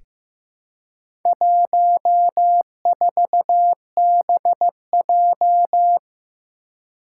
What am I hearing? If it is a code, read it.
Morse code, heard as a keyed tone, about 15 wpm, 700 Hz. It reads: 14BJ
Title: Four Challenging Letter-Number Combinations 15wpm